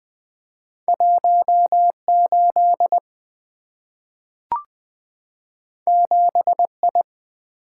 Morse code, heard as a keyed tone, about 20 wpm, 700 Hz. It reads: 18 7I
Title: Two Letter-Number Combinations - 20wpm